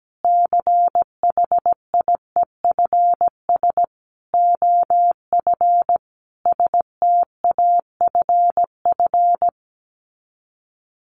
Morse code, heard as a keyed tone, about 17 wpm, 700 Hz. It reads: CHIEFS OF STAFF
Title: Sets of 5 Words Encore2 17wpm